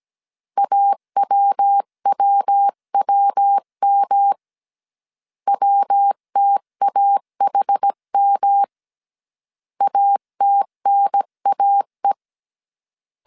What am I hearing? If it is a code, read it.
Morse code, heard as a keyed tone, about 17 wpm, 800 Hz. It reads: AWWWM WTAHM ATNAE